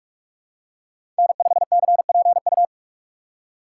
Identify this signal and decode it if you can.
Morse code, heard as a keyed tone, about 46 wpm, 700 Hz. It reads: N5CPU
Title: USA Call Signs - 5 Characters - 3x Character Spacing - 45wpm